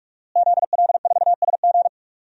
Morse code, heard as a keyed tone, about 41 wpm, 700 Hz. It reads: ZL4SG
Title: DXCC-100 Call Signs - 5 Characters - Repeated 3x - 45wpm